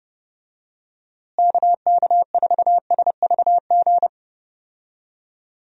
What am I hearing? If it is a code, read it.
Morse code, heard as a keyed tone, about 32 wpm, 700 Hz. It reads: KK4SVG
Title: USA Call Signs - 6 Characters - Repeated 3x - 30wpm